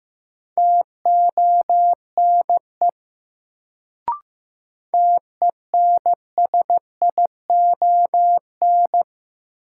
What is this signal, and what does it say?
Morse code, heard as a keyed tone, about 15 wpm, 700 Hz. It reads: TONE TENSION
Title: Sentences from Top 1600 Words - New Word Drill - 15wpm